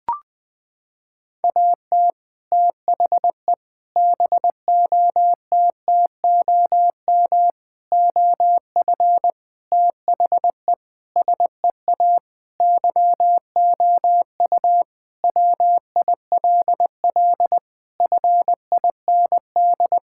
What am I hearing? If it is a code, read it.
Morse code, heard as a keyed tone, about 20 wpm, 700 Hz. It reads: AT THE BOTTOM OF THE SEA YOU WILL FIND
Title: Sentences with 3-Word Phrases - 20wpm